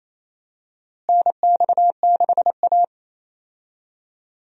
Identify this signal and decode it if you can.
Morse code, heard as a keyed tone, about 28 wpm, 700 Hz. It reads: NX6A
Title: USA Call Signs - 4 Characters - Repeated 3x - 28wpm